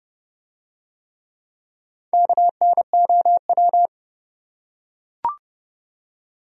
Morse code, heard as a keyed tone, about 28 wpm, 700 Hz. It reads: KNOW